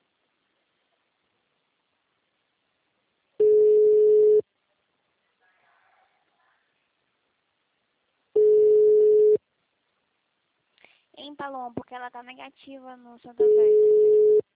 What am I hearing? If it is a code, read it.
Ringback tone (Europe, 425 Hz), with speech.